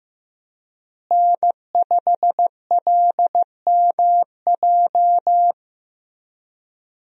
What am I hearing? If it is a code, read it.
Morse code, heard as a keyed tone, about 16 wpm, 700 Hz. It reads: N5LMJ
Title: USA Call Signs - 5 Characters - Repeated 3x - 15wpm